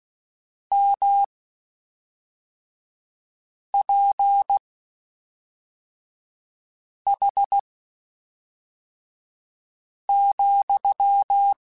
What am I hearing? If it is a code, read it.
Morse code, heard as a keyed tone, about 16 wpm, spaced as for 3 wpm, 800 Hz. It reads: MPH,
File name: Spaced Characters / Spaced Characters-mp3s / Weather5 5 sp @5